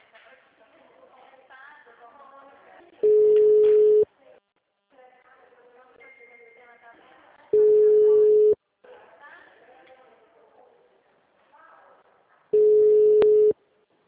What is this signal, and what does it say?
Ringback tone (Europe, 425 Hz), with speech.